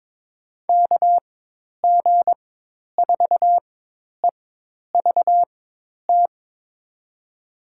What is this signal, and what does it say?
Morse code, heard as a keyed tone, about 22 wpm, spaced as for 10 wpm, 700 Hz. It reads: KG4EVT
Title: USA Call Signs - 4x Character Spacing - 22wpm